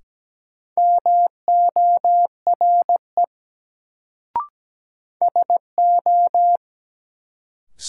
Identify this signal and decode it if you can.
Morse code, heard as a keyed tone, about 17 wpm, 700 Hz. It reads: MORE SO
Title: Sentences from Top 100 Words - New Word Drill - 17wpm